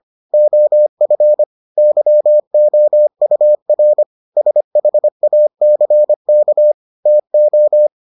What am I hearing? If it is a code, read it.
Morse code, heard as a keyed tone, about 25 wpm, 600 Hz. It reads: OF YOUR SHACK TO